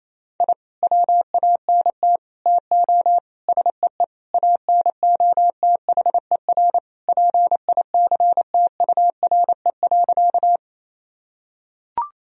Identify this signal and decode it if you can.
Morse code, heard as a keyed tone, about 28 wpm, 700 Hz. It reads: I WANT TO SEE ANOTHER PICTURE.
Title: Sentences from Top 1000 Words - Review Part 1 - 28wpm